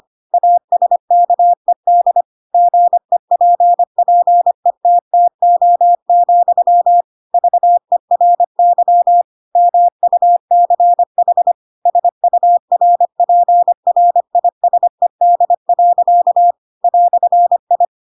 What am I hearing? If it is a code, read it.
Morse code, heard as a keyed tone, about 25 wpm, 700 Hz. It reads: ASKED GEPPETTO, VERY MUCH SURPRISED. "I
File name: Pinocchio / Pinocchio ch7 25 25 ik1hge